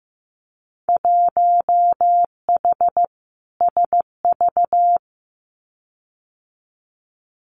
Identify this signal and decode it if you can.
Morse code, heard as a keyed tone, about 15 wpm, 700 Hz. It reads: 1H SV
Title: Winter Field Day Exchanges 15wpm